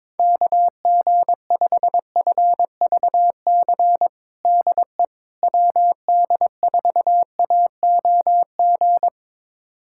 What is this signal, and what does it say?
Morse code, heard as a keyed tone, about 22 wpm, 700 Hz. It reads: KG5FVC DE WD4AOG
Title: USA Callsign DE Callsign 22wpm